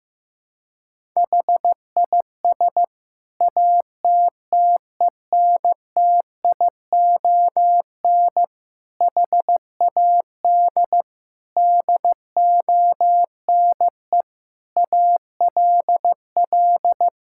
Morse code, heard as a keyed tone, about 15 wpm, 700 Hz. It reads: HIS ATTENTION HAD DONE ALL
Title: Sentences from Top 1000 Words - Review Part 6 - 15wpm